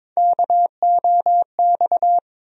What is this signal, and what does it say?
Morse code, heard as a keyed tone, about 23 wpm, 700 Hz. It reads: KOX
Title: USA Call Sign Postfixes - 3 Letters 22wpm